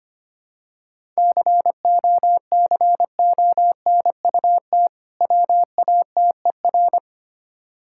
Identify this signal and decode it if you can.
Morse code, heard as a keyed tone, about 25 wpm, 700 Hz. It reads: COCONUT WATER
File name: practice / 25wpm / Difficult Words with Hints - 25wpm